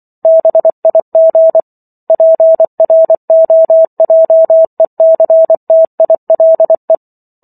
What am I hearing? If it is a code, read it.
Morse code, heard as a keyed tone, about 24 wpm, 650 Hz. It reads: BIG PROJECTILE